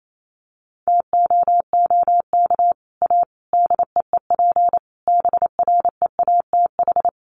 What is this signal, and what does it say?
Morse code, heard as a keyed tone, about 28 wpm, 700 Hz. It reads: TOOK A DEEP BREATH
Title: Sets of 5 Words Encore 28wpm